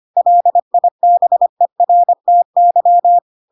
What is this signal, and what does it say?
Morse code, heard as a keyed tone, about 25 wpm, 700 Hz. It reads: LIBERTY
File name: Pinocchio / Pinocchio ch3 25 25 ik1hge